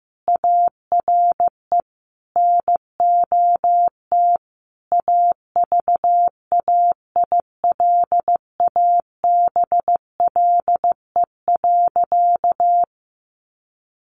Morse code, heard as a keyed tone, about 15 wpm, 700 Hz. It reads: ARE NOT AVAILABLE.
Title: Sentences from Top 600 Words 15wpm